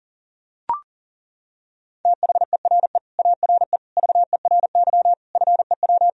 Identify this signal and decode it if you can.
Morse code, heard as a keyed tone, about 40 wpm, 700 Hz. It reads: THERE ARE VERY FEW
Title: Sentences from Top 1000 Words - Review Part 1 - 40wpm